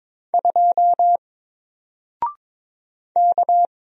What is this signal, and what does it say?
Morse code, heard as a keyed tone, about 22 wpm, 700 Hz. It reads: 2 K
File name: Single-Letter-Number-22wpm